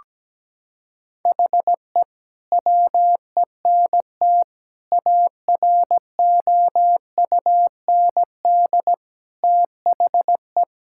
Morse code, heard as a keyed tone, about 17 wpm, 700 Hz. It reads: HE WENT AROUND THE
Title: Sentences from Top 1000 Words - Review Part 5 - 17wpm